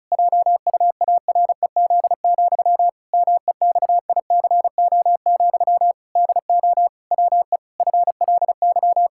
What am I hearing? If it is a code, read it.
Morse code, heard as a keyed tone, about 34 wpm, 700 Hz. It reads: JUAREZ, MEXICO, DO WE FLY
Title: Pangram Challenge v2 - 35wpm